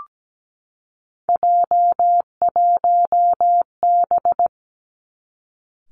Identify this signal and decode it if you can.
Morse code, heard as a keyed tone, about 17 wpm, 700 Hz. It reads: J1B
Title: Three Challenging Letter-Number Combinations 17wpm